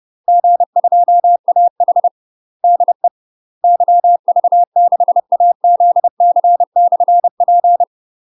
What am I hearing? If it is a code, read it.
Morse code, heard as a keyed tone, about 30 wpm, 700 Hz. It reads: G2AH DE YV6AZC/P
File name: UK MASTER QSOs TRAINING / 30WPM / UK MASTER QSOs 0022